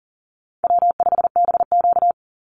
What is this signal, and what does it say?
Morse code, heard as a keyed tone, about 40 wpm, 700 Hz. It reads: W5BQ
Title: Easy Call Signs - 40wpm